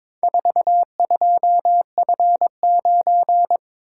Morse code, heard as a keyed tone, about 22 wpm, 700 Hz. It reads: 42F9
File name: practice / 22wpm / Four Letter-Number Combinations - 22wpm